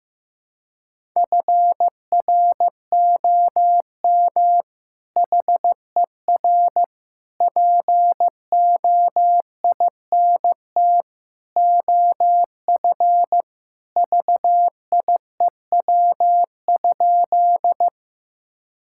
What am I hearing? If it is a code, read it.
Morse code, heard as a keyed tone, about 15 wpm, 700 Hz. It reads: FROM HER POINT OF VIEW?
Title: Sentences from Top 1000 Words - Review Part 6 - 15wpm